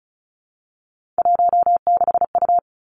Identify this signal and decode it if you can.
Morse code, heard as a keyed tone, about 34 wpm, 700 Hz. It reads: 16U
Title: Three Challenging Letter-Number Combinations 35wpm